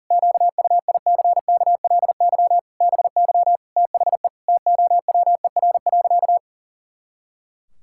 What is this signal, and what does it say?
Morse code, heard as a keyed tone, about 40 wpm, 700 Hz. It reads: QUICKLY BY THE TOWER.
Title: Pangram Challenge v2 - 40wpm